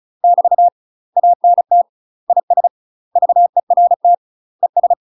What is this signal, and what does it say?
Morse code, heard as a keyed tone, about 36 wpm, 700 Hz. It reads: <BT> ANT IS VERT ES
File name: UK MASTER QSOs TRAINING / 35WPM / UK MASTER QSOs 0023